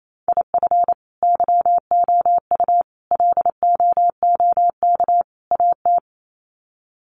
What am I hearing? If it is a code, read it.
Morse code, heard as a keyed tone, about 28 wpm, 700 Hz. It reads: IF YOU LOOK AT
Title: Sets of 4 Words 28wpm